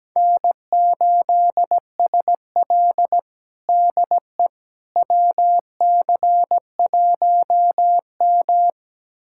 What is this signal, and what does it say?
Morse code, heard as a keyed tone, about 17 wpm, 700 Hz. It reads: N8SL DE WC1M
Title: USA Callsign DE Callsign - 4 Characters - 17wpm